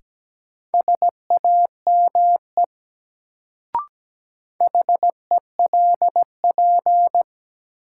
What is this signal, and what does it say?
Morse code, heard as a keyed tone, about 17 wpm, 700 Hz. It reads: SAME HELP
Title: Sentences from Top 200 Words - New Word Drill - 17wpm